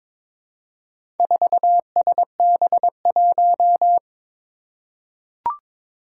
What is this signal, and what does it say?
Morse code, heard as a keyed tone, about 22 wpm, 700 Hz. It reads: 4SB1
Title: Four Challenging Letter-Number Combinations 22wpm